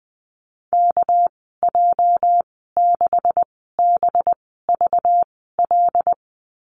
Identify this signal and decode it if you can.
Morse code, heard as a keyed tone, about 19 wpm, spaced as for 14 wpm, 700 Hz. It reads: KJ6BVL
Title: 219 Lesson 35g - Number 0 - Call Signs 20wpm